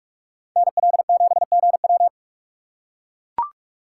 Morse code, heard as a keyed tone, about 44 wpm, 700 Hz. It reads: NL7GW